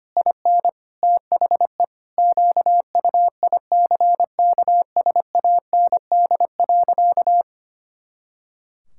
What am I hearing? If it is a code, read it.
Morse code, heard as a keyed tone, about 25 wpm, 700 Hz. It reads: IN THE QUICKSAND.
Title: Pangram Challenge v2 - 25wpm